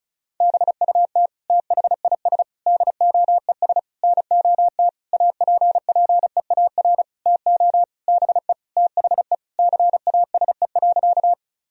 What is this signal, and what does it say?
Morse code, heard as a keyed tone, about 36 wpm, 700 Hz. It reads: BUT THIS DOES NOT APPEAR TO BE THE CASE.
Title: Sentences from Top 1000 Words - Review Part 2 - 35wpm